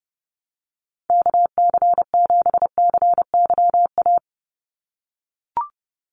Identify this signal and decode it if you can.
Morse code, heard as a keyed tone, about 30 wpm, 700 Hz. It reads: KC7CYA
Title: USA Call Signs 30wpm